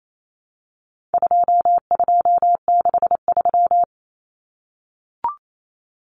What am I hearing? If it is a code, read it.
Morse code, heard as a keyed tone, about 28 wpm, 700 Hz. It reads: 2263
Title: Common Cut Numbers - 4-Digit 28wpm